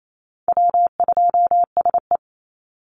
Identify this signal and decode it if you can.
Morse code, heard as a keyed tone, about 28 wpm, 700 Hz. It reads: W2SE
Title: NA SOTA Chaser Callsigns 28wpm